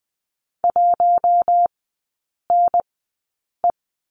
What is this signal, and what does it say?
Morse code, heard as a keyed tone, about 20 wpm, spaced as for 8 wpm, 700 Hz. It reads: 1NE
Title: 041 Lesson 8e - Number 1 - Three Characters 20wpm